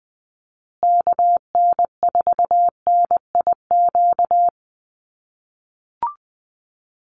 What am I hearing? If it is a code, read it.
Morse code, heard as a keyed tone, about 20 wpm, 700 Hz. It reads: KN4NIQ
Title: USA Call Signs Repeated 3x 20wpm